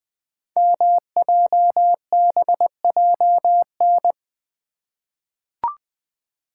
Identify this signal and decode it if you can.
Morse code, heard as a keyed tone, about 20 wpm, 700 Hz. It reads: MJBJN
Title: Code Groups - 5 Letters - 20wpm